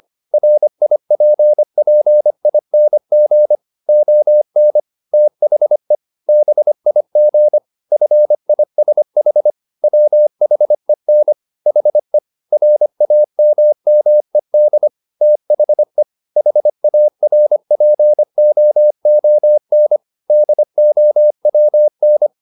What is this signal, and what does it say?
Morse code, heard as a keyed tone, about 25 wpm, 600 Hz. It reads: RIPPING ON THE BIG FISH WHEN HE RAMMED THE HARPOON DOWN